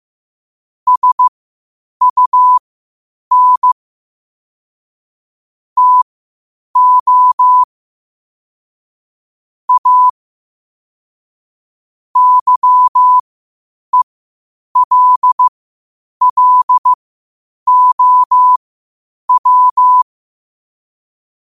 Morse code, heard as a keyed tone, about 15 wpm, spaced as for 8 wpm, 1000 Hz. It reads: SUN TO A YELLOW
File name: qotdfiles / 8wpm